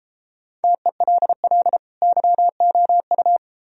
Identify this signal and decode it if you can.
Morse code, heard as a keyed tone, about 33 wpm, 700 Hz. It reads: TELL YOU